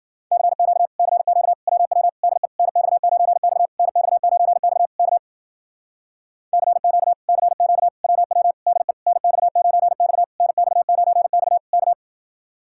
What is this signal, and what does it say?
Morse code, heard as a keyed tone, about 58 wpm, 700 Hz. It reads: CQ CQ WW DE NX9X NX9X K CQ CQ WW DE NX9X NX9X K